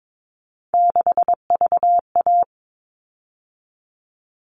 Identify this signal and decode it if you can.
Morse code, heard as a keyed tone, about 22 wpm, 700 Hz. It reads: 6VA
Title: All Cut Numbers - 3-Digit 22wpm